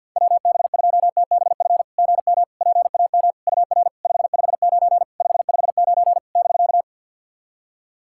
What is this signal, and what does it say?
Morse code, heard as a keyed tone, about 51 wpm, 690 Hz. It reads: WB2TBF GM PAM UR 559 559 <BK>
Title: SOTA QSO Practice 50wpm